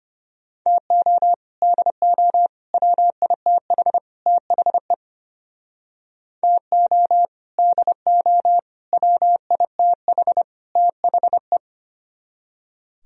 Morse code, heard as a keyed tone, about 32 wpm, 700 Hz. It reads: TO DO WITH THE TO DO WITH THE